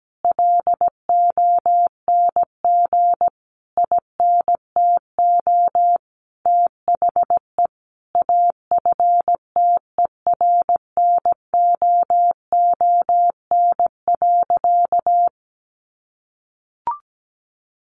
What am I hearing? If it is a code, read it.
Morse code, heard as a keyed tone, about 17 wpm, 700 Hz. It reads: LONG INTO THE AFTERNOON.
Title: Sentences from Top 1500 Words 17wpm